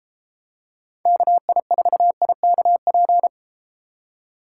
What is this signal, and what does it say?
Morse code, heard as a keyed tone, about 32 wpm, 700 Hz. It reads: KI4IKP